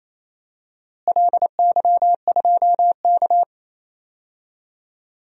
Morse code, heard as a keyed tone, about 28 wpm, 700 Hz. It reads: LY2K
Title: DXCC-100 Call Signs - 4 Characters - Repeated 3x - 28wpm